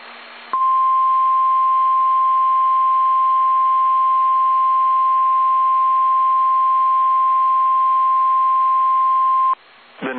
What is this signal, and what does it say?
Emergency Alert System (SAME) tones (the 1050 Hz attention tone). It is heard from 0.5 to 9.6 s.